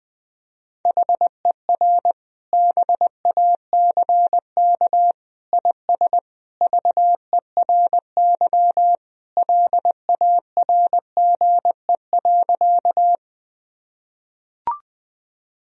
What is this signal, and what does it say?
Morse code, heard as a keyed tone, about 20 wpm, 700 Hz. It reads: HER BACK IS VERY LARGE.
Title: Sentences from Top 1000 Words - Review Part 1 - 20wpm